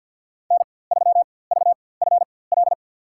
Morse code, heard as a keyed tone, about 47 wpm, spaced as for 23 wpm, 700 Hz. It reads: N3VFL
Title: USA Call Signs - 5 Characters - 4x Character Spacing - 50wpm